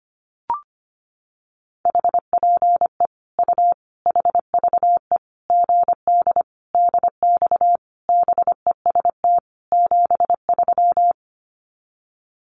Morse code, heard as a keyed tone, about 25 wpm, 700 Hz. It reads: HPE U HVE GD DX BEST 73
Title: Common QSO Phrases 25wpm